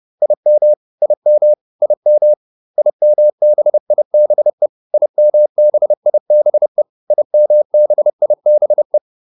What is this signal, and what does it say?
Morse code, heard as a keyed tone, about 30 wpm, 600 Hz. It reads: IM IM IM IMBIBE IMBIBE IMBIBE
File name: IM6 30